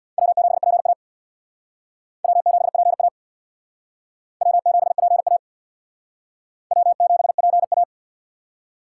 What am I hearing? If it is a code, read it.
Morse code, heard as a keyed tone, about 60 wpm, 700 Hz. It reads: W7PA W7PA W7PA W7PA
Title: Speed Racing - USA Call Signs 50wpm